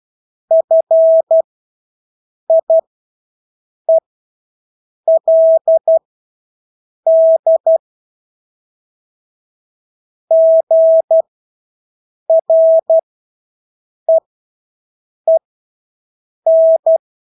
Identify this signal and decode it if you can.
Morse code, heard as a keyed tone, about 12 wpm, spaced as for 6 wpm, 650 Hz. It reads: FIELD GREEN
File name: Engelske ord 12-6 wpm